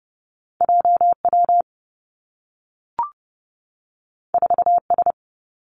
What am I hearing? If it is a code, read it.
Morse code, heard as a keyed tone, about 30 wpm, 700 Hz. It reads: J W 4 S
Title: Two Challenging Letter-Number Combinations 30wpm